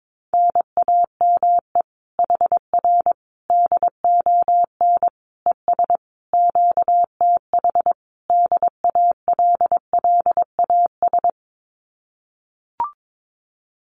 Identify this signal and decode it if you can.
Morse code, heard as a keyed tone, about 22 wpm, 700 Hz. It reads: NAME HR DON ES QTH DALLAS
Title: Common QSO Phrases Encore 22wpm